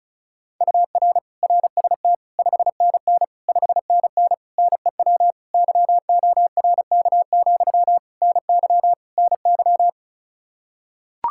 Morse code, heard as a keyed tone, about 36 wpm, 700 Hz. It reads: UR RST 5NN 5NN NEW YORK, NY NY
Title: Practice QSO - Exchange 1 - Beginning - 35wpm